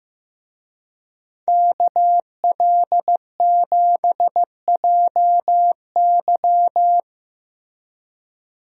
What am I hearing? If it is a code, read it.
Morse code, heard as a keyed tone, about 15 wpm, 700 Hz. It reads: KL7JY
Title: DXCC-100 Call Signs - 5 Characters - Repeated 2x - 15wpm